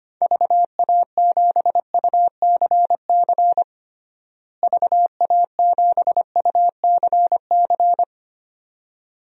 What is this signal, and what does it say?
Morse code, heard as a keyed tone, about 25 wpm, 700 Hz. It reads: VA7UCC VA7UCC
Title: DXCC-100 Call Signs - 6 Characters - Repeated 2x - 25wpm